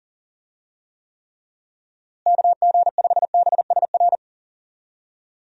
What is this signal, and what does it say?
Morse code, heard as a keyed tone, about 40 wpm, 700 Hz. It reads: KG5BSR
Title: USA Call Signs - 6 Characters - Repeated 3x - 40wpm